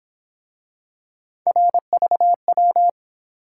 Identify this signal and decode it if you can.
Morse code, heard as a keyed tone, about 27 wpm, 700 Hz. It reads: RVW
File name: Speed-Racing - Three Letter-Number Combinations - 22wpm